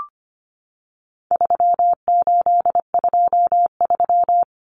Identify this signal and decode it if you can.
Morse code, heard as a keyed tone, about 25 wpm, 700 Hz. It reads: 3823
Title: Common Cut Numbers - 4-Digit 25wpm